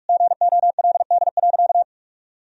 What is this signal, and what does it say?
Morse code, heard as a keyed tone, about 42 wpm, 700 Hz. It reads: GOLD.
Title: Sentences from Top 1000 Words - Review Part 4 - 45wpm